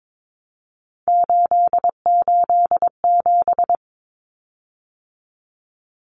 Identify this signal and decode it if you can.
Morse code, heard as a keyed tone, about 22 wpm, 700 Hz. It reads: 887
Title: Common Cut Numbers - 3-Digit 22wpm